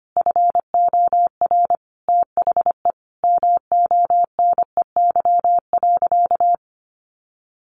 Morse code, heard as a keyed tone, about 25 wpm, 700 Hz. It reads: FOR THE MONEY.
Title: Sentences from Top 1300 Words 25wpm